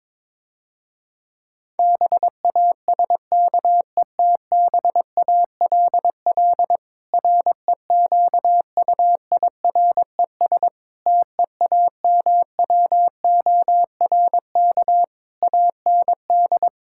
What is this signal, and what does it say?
Morse code, heard as a keyed tone, about 22 wpm, 700 Hz. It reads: BASKETBALL REQUIRES TEAMWORK AND
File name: Difficult Sentences with Hints - 22wpm